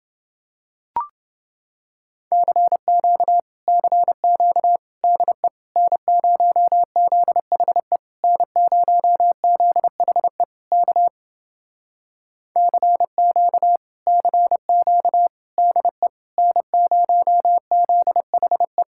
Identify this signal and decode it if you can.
Morse code, heard as a keyed tone, about 30 wpm, 700 Hz. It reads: CQ CQ DE N0ZHE N0ZHE K CQ CQ DE N0ZHE